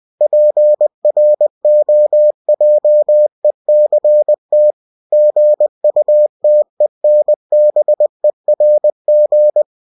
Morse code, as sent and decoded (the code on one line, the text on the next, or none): .--. .-. --- .--- . -.-. - / --. ..- - . -. -... .-. --.
PROJECT GUTENBERG